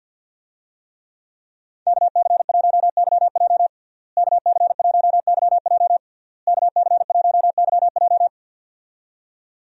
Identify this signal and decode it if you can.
Morse code, heard as a keyed tone, about 51 wpm, 700 Hz. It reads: KC1YJ KC1YJ KC1YJ
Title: USA Call Signs - 5 Characters - Repeated 3x - 50wpm